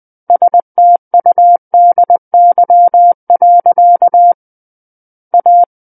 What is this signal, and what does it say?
Morse code, heard as a keyed tone, about 20 wpm, 700 Hz. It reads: STUDY. A
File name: 20 WPM